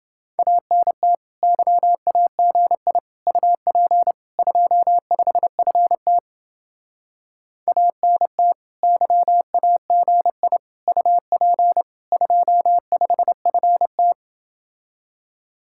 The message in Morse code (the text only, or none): ANT YAGI UP 25FT ANT YAGI UP 25FT